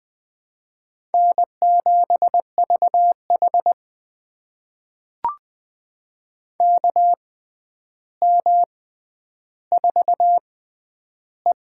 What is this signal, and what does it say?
Morse code, heard as a keyed tone, about 20 wpm, spaced as for 10 wpm, 700 Hz. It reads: N7VH KM4E
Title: USA Call Signs - 4 Characters - 6x Character Spacing - 20wpm